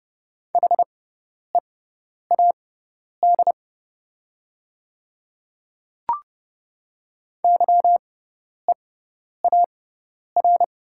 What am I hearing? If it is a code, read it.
Morse code, heard as a keyed tone, about 31 wpm, spaced as for 10 wpm, 700 Hz. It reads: HEAD YEAR
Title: 6x Character Spacing - Top 500 Words - 30wpm